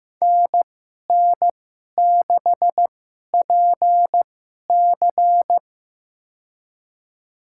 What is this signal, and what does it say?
Morse code, heard as a keyed tone, about 15 wpm, spaced as for 11 wpm, 700 Hz. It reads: NN6PC
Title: USA Call Signs - 5 Characters - 2x Character Spacing - 15wpm